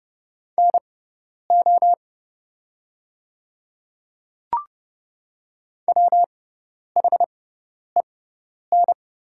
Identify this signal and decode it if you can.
Morse code, heard as a keyed tone, about 31 wpm, spaced as for 10 wpm, 700 Hz. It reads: NO WHEN
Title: 6x Character Spacing - Top 100 Words - 30wpm